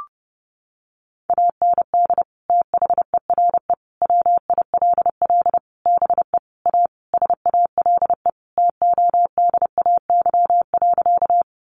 Morse code, heard as a keyed tone, about 30 wpm, 700 Hz. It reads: AND THERE WILL BE A SALE TODAY.
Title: Sentences from Top 1400 Words 30wpm